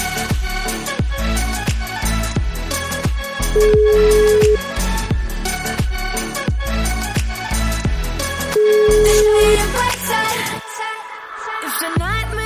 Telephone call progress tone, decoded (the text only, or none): ringback tone